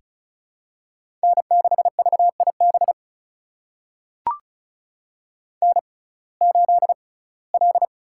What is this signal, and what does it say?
Morse code, heard as a keyed tone, about 34 wpm, spaced as for 21 wpm, 700 Hz. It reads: N6VIB N8L
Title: USA Call Signs - 5 Characters - 6x Character Spacing - 35wpm